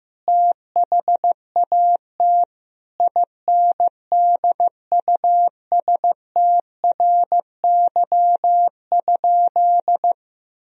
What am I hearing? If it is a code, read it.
Morse code, heard as a keyed tone, about 15 wpm, 700 Hz. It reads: THAT INDUSTRY?
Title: Sentences from Top 1000 Words - Review Part 4 - 15wpm